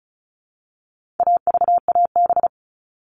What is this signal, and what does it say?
Morse code, heard as a keyed tone, about 34 wpm, 700 Hz. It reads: AVAB
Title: All Cut Numbers - 4-Digit 35wpm